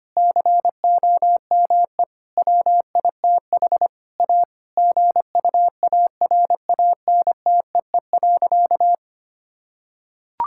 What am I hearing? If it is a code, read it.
Morse code, heard as a keyed tone, about 25 wpm, 700 Hz. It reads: COME WITH A GUARANTEE.
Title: Sentences from Top 1600 Words 25wpm